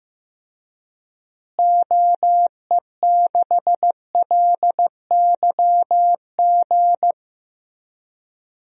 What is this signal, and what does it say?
Morse code, heard as a keyed tone, about 15 wpm, 700 Hz. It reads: OE6LYG
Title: DXCC-100 Call Signs - 6 Characters - Repeated 2x - 15wpm